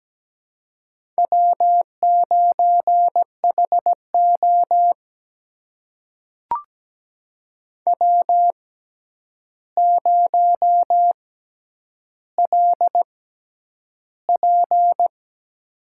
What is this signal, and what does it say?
Morse code, heard as a keyed tone, about 17 wpm, spaced as for 9 wpm, 700 Hz. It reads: W9HO W0LP
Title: USA Call Signs - 4 Characters - 6x Character Spacing - 17wpm